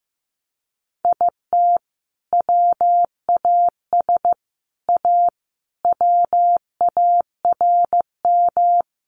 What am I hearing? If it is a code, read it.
Morse code, heard as a keyed tone, about 15 wpm, 700 Hz. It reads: IT WAS A WARM
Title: Sentences from Top 1500 Words 15wpm